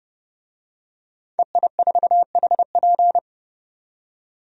Morse code, heard as a keyed tone, about 30 wpm, 700 Hz. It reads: EI4HP
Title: DXCC-100 Call Signs - 5 Characters - 30wpm